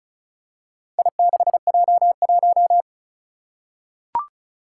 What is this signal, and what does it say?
Morse code, heard as a keyed tone, about 35 wpm, 700 Hz. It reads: I6J1
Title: Four Challenging Letter-Number Combinations 35wpm